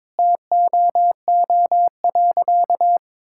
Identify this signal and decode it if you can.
Morse code, heard as a keyed tone, about 22 wpm, 700 Hz. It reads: TOO.
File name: practice / 22wpm / Sentences with 8-Word Phrases - 22wpm